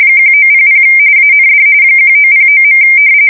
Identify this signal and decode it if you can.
RTTY, about 45.45 baud. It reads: 1900 AND 1909.
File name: fsk170 / battleships